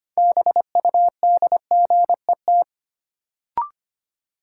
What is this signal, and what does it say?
Morse code, heard as a keyed tone, about 25 wpm, 700 Hz. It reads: BUDGET
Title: Sentences from Top 1200 Words - New Word Drill - 25wpm